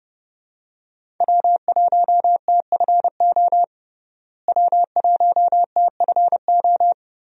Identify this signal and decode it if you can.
Morse code, heard as a keyed tone, about 30 wpm, 700 Hz. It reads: W1TFO W1TFO
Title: USA Call Signs - 5 Characters - Repeated 2x - 30wpm